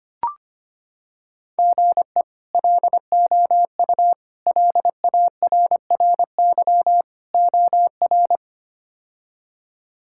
Morse code, heard as a keyed tone, about 25 wpm, 700 Hz. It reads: GE LOU LARRY OR
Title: K1USN SST QSO Practice - 25wpm